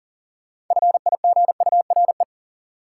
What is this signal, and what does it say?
Morse code, heard as a keyed tone, about 40 wpm, 700 Hz. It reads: FIGURE
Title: Sets of 4 Words - New Word Drill - 40wpm